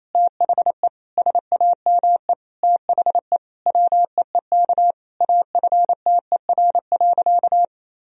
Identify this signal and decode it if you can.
Morse code, heard as a keyed tone, about 28 wpm, 700 Hz. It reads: THE SAME THE WEEK AFTER.
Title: Sentences from Top 500 Words and Abbreviations 28wpm